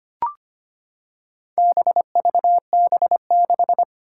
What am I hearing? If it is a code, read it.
Morse code, heard as a keyed tone, about 25 wpm, 700 Hz. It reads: BVB6
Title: Four Challenging Letter-Number Combinations 25wpm